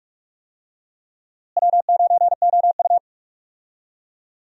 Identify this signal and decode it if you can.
Morse code, heard as a keyed tone, about 44 wpm, 700 Hz. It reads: W9OU
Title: USA Call Signs - 4 Characters - Repeated 2x - 45wpm